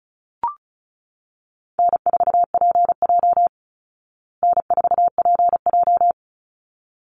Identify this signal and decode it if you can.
Morse code, heard as a keyed tone, about 34 wpm, 700 Hz. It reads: N4PJ N4PJ
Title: Call Signs Repeated 3x - 35wpm